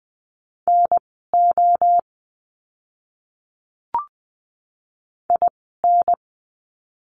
Morse code, heard as a keyed tone, about 20 wpm, spaced as for 15 wpm, 700 Hz. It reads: NO IN
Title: 030 Lesson 6f - Letter i - Words 20wpm